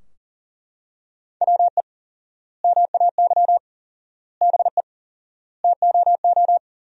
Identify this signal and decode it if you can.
Morse code, heard as a keyed tone, about 40 wpm, 700 Hz. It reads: WE MAY BE TOO
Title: 4x Word-Spacing - Sentences from Top 300 Words - 40wpm